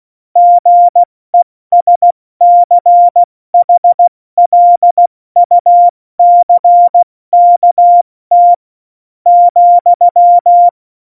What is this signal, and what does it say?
Morse code, heard as a keyed tone, about 16 wpm, spaced as for 13 wpm, 700 Hz. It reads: GESCHLUCKT,
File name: Morsekurs / Witze / Witz 16. Januar 2018, 70 F80